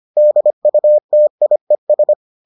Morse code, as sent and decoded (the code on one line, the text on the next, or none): -.. ..- - .. ...
DUTIES